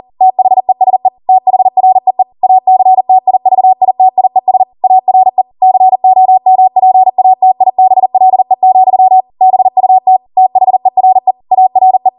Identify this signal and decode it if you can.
Morse code, heard as a keyed tone, about 40 wpm, 750 Hz. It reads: THESE THREE ACTIVITIES ARE COMPATIBLE, BUT THERE ARE